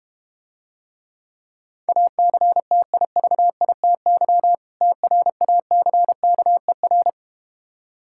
Morse code, heard as a keyed tone, about 32 wpm, 700 Hz. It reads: ACTIVITY TRACKER